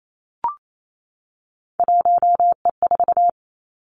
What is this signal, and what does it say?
Morse code, heard as a keyed tone, about 28 wpm, 700 Hz. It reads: 1E4
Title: Three Challenging Letter-Number Combinations 28wpm